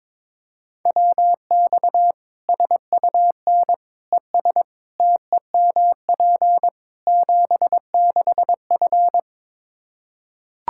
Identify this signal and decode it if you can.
Morse code, heard as a keyed tone, about 22 wpm, 700 Hz. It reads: WX SUN ES TEMP 76F
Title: K1USN SST QSO Practice 22wpm